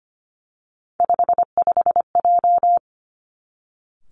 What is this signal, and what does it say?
Morse code, heard as a keyed tone, about 25 wpm, 700 Hz. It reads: 55J
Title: Three Challenging Letter-Number Combinations 25wpm